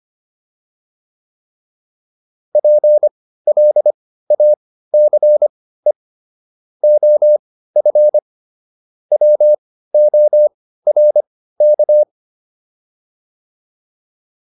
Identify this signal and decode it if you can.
Morse code, heard as a keyed tone, about 25 wpm, spaced as for 15 wpm, 600 Hz. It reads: PLACE OF WORK